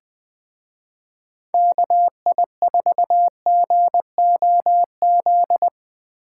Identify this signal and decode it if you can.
Morse code, heard as a keyed tone, about 20 wpm, 700 Hz. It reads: KI4GOZ
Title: Speed Racing - USA 2x3 Call Signs 17wpm